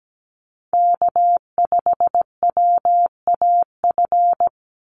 Morse code, heard as a keyed tone, about 17 wpm, 700 Hz. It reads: K5WAF
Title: Call Signs Repeated 2x - 17wpm